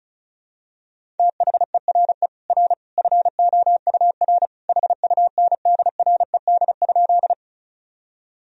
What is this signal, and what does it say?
Morse code, heard as a keyed tone, about 36 wpm, 700 Hz. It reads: THERE R FOUR HUNDRED?
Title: Sentences from Top 400 Words and Abbreviations 35wpm